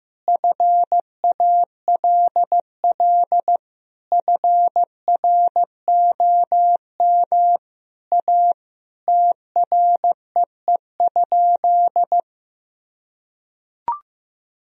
Morse code, heard as a keyed tone, about 15 wpm, 700 Hz. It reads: FALL FROM A TREE?
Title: Sentences from Top 1000 Words - Review Part 2 - 15wpm